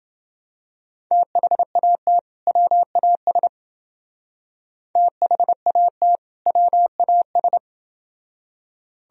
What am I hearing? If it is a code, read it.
Morse code, heard as a keyed tone, about 28 wpm, 700 Hz. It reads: THAT WAS THAT WAS